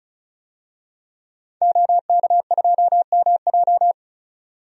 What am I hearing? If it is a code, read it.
Morse code, heard as a keyed tone, about 36 wpm, 700 Hz. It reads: OK2MJ
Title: DXCC-100 Call Signs - 5 Characters - 35wpm